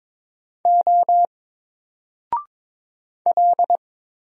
Morse code, heard as a keyed tone, about 22 wpm, 700 Hz. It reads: O L
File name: Single-Letter-Number-22wpm